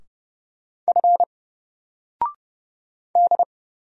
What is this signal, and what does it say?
Morse code, heard as a keyed tone, about 31 wpm, 700 Hz. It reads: F D
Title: Single Character (Letter:Number:Punctuation:Prosign) 30wpm